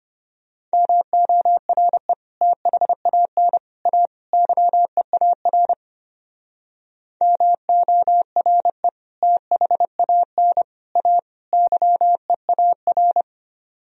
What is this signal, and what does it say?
Morse code, heard as a keyed tone, about 31 wpm, 700 Hz. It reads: MORE THAN A YEAR MORE THAN A YEAR
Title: Speed-Racing - Sets of 4 Words 25wpm